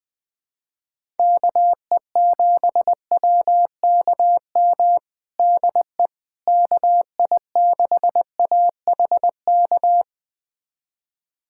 Morse code, heard as a keyed tone, about 21 wpm, 700 Hz. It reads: KE7WKM DE KI6AHK
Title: USA Callsign DE Callsign - 6 Characters - 20wpm